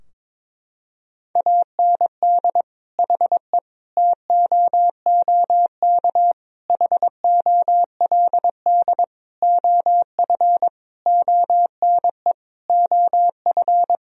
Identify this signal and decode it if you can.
Morse code, heard as a keyed tone, about 22 wpm, 700 Hz. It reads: AND HE TOOK HOLD OF ONE OF
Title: Sentences from Top 1000 Words - Review Part 4 - 22wpm